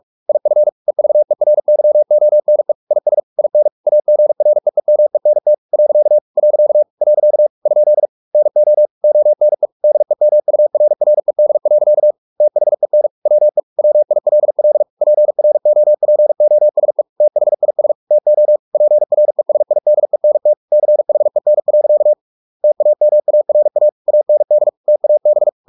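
Morse code, heard as a keyed tone, about 44 wpm, 600 Hz. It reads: IF EVERYONE IS IN AGREEMENT . . ? NO ONE DEMURRED. THEN WE WILL PROPOSE THIS TO PRESIDENT CHEN. TAMARA AND TAB
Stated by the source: Morse practice at 45 wpm, the novel Never by Ken Follett in English, from CW med Gnister.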